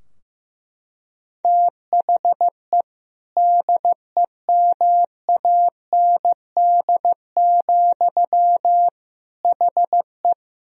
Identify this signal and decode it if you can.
Morse code, heard as a keyed tone, about 15 wpm, 700 Hz. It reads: THE DEMAND, HE
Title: Sentences from Top 1000 Words - Review Part 4 - 15wpm